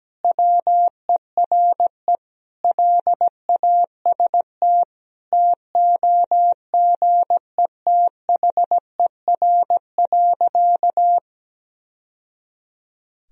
Morse code, heard as a keyed tone, about 17 wpm, 700 Hz. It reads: WERE LAST TOGETHER.
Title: Sentences from Top 1000 Words - Review Part 1 - 17wpm